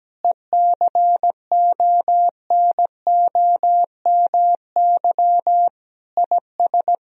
Morse code, heard as a keyed tone, about 17 wpm, 700 Hz. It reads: ECONOMY IS
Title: Sentences from Top 1000 Words - Review Part 3 - 17wpm